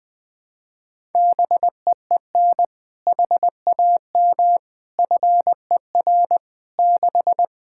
Morse code, heard as a keyed tone, about 21 wpm, 700 Hz. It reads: BEEN HAM FER 6
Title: Practice QSO - Exchange 3 - More Info - 20wpm